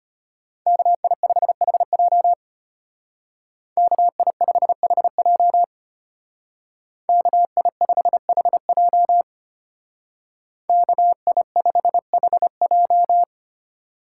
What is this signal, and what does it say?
Morse code, heard as a keyed tone, about 30 wpm, 700 Hz. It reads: KI5HJ KI5HJ KI5HJ KI5HJ
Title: Speed Racing - USA 2x2 Call Signs 25wpm